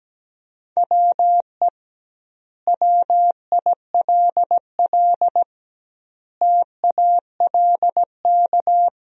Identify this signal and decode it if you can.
Morse code, heard as a keyed tone, about 17 wpm, 700 Hz. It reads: WE WILL TALK
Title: 2x Word-Spacing - Sentences from Top 400 Words - 17wpm